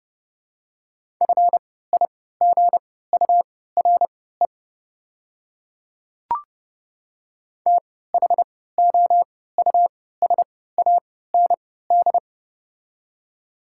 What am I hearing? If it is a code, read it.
Morse code, heard as a keyed tone, about 31 wpm, spaced as for 17 wpm, 700 Hz. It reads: FIGURE THOUSAND
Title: 3x Character Spacing - Top 500 Words - 30wpm